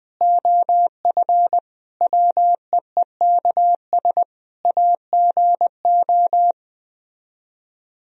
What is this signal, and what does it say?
Morse code, heard as a keyed tone, about 20 wpm, 700 Hz. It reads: OF WEEKS AGO
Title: Speed-Racing - Sets of 5 Words 17wpm